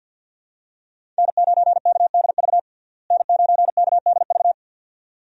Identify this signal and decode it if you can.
Morse code, heard as a keyed tone, about 51 wpm, 700 Hz. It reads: N9KDV N9KDV
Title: USA Call Signs - 5 Characters - Repeated 2x - 50wpm